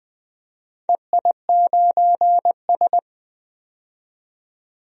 Morse code, heard as a keyed tone, about 20 wpm, 700 Hz. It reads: EI9S
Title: DXCC-100 Call Signs - 4 Characters - 20wpm